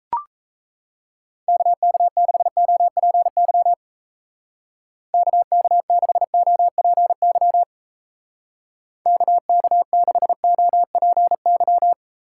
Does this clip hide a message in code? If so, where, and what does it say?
Yes: Morse code from 1.5 to 11.9 s, heard as a keyed tone, about 37 wpm, 700 Hz. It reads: KK6OPY KK6OPY KK6OPY